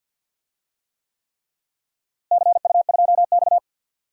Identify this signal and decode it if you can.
Morse code, heard as a keyed tone, about 47 wpm, 700 Hz. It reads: CU2X
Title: DXCC-100 Call Signs - 4 Characters - Repeated 3x - 50wpm